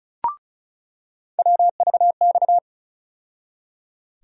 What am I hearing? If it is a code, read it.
Morse code, heard as a keyed tone, about 34 wpm, 700 Hz. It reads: WVX
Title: USA Call Sign Postfixes - 3 Letters 35wpm